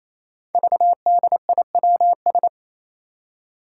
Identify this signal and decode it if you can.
Morse code, heard as a keyed tone, about 28 wpm, 700 Hz. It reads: VDIWS
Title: Five Challenging Letter-Number Combinations 28wpm